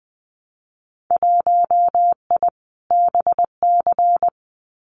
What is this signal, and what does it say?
Morse code, heard as a keyed tone, about 20 wpm, 700 Hz. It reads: 1I BC
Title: Winter Field Day Exchanges Encore 20wpm